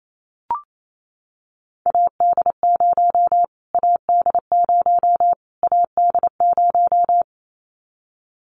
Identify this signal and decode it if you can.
Morse code, heard as a keyed tone, about 28 wpm, 700 Hz. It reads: AD0 AD0 AD0
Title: Callsign Prefixes for Top 25 DXCC - Repeated 3x - 28wpm